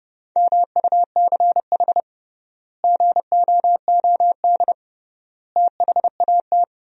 Morse code, heard as a keyed tone, about 30 wpm, 700 Hz. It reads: MUCH GOOD THAT
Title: 3x Word-Spacing - Sentences from Top 200 Words - 30wpm